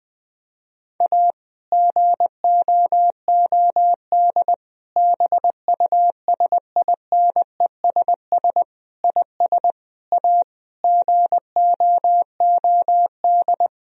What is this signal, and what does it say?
Morse code, heard as a keyed tone, about 20 wpm, 700 Hz. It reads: A GOOD BUSINESS IS A GOOD
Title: Sentences from Top 1000 Words - Review Part 4 - 20wpm